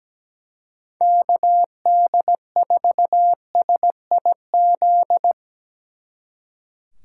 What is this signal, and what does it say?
Morse code, heard as a keyed tone, about 19 wpm, 700 Hz. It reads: KD4SIZ
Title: Speed Racing - USA 2x3 Call Signs 17wpm